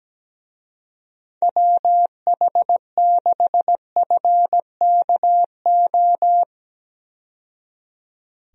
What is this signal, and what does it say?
Morse code, heard as a keyed tone, about 17 wpm, 700 Hz. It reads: WH6FKO
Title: USA Call Signs - 6 Characters - 17wpm